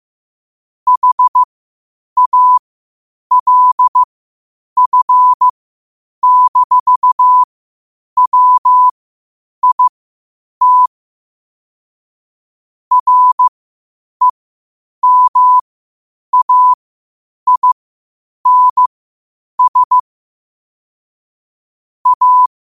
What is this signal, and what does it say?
Morse code, heard as a keyed tone, about 15 wpm, spaced as for 8 wpm, 1000 Hz. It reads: HALF-WIT REMAINS A
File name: qotdfiles / 8wpm